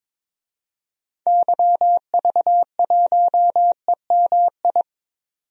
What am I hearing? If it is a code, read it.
Morse code, heard as a keyed tone, about 22 wpm, 700 Hz. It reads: YV1EMI
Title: DXCC-100 Call Signs - 6 Characters - 22wpm